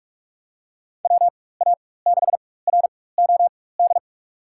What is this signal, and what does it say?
Morse code, heard as a keyed tone, about 45 wpm, spaced as for 21 wpm, 700 Hz. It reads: WA6ROD
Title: USA Call Signs - 6 Characters - 4x Character Spacing - 45wpm